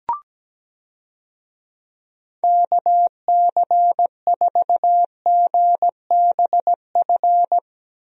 Morse code, heard as a keyed tone, about 17 wpm, 700 Hz. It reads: KC4GBF
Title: USA Call Signs - 6 Characters - Repeated 3x - 17wpm